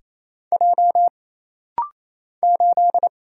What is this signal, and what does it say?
Morse code, heard as a keyed tone, about 28 wpm, 700 Hz. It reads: J 8
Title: Single Letter-Number - Rapid-Fire 28wpm